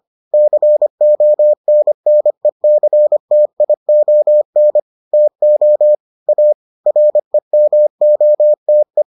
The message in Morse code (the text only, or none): CONNECTION TO A REMOTE